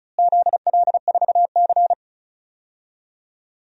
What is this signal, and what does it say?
Morse code, heard as a keyed tone, about 36 wpm, 700 Hz. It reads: ZL4C
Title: DXCC-100 Call Signs - 4 Characters - Repeated 2x - 35wpm